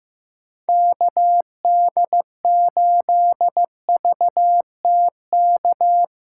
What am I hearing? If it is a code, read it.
Morse code, heard as a keyed tone, about 16 wpm, 700 Hz. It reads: KD8VTK
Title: USA Callsign DE Callsign 15wpm